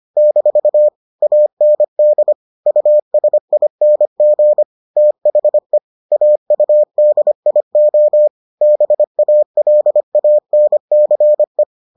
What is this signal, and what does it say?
Morse code, heard as a keyed tone, about 25 wpm, 600 Hz. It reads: - AND USING THE AUDIO BALANCE